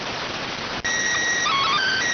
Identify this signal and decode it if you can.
SSTV (slow-scan television), PD 290.